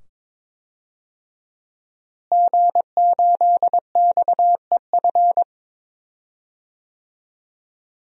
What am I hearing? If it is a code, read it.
Morse code, heard as a keyed tone, about 22 wpm, 700 Hz. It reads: G8XEF
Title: DXCC-100 Call Signs - 5 Characters - Repeated 2x - 22wpm